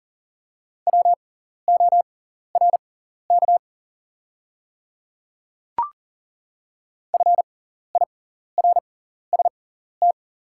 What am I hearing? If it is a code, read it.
Morse code, heard as a keyed tone, about 41 wpm, spaced as for 14 wpm, 690 Hz. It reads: WORK FIRST
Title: 6x Character Spacing - Top 300 Words - 40wpm